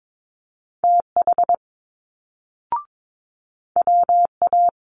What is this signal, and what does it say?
Morse code, heard as a keyed tone, about 22 wpm, 700 Hz. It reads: TH WA
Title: Multi-letter Phonemes Quick Fire 22wpm